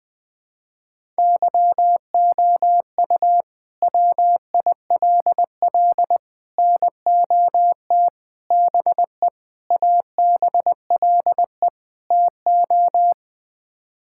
Morse code, heard as a keyed tone, about 20 wpm, 700 Hz. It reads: YOU WILL NOT BE ABLE TO
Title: Sets of 6 Words - 20wpm